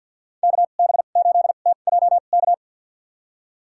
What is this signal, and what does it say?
Morse code, heard as a keyed tone, about 53 wpm, spaced as for 37 wpm, 700 Hz. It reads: KB8TJK
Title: USA Call Signs - 2x Character Spacing - 50wpm